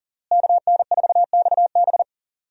Morse code, heard as a keyed tone, about 40 wpm, 700 Hz. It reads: KN4XB
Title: USA Call Signs - 5 Characters - Repeated 3x - 40wpm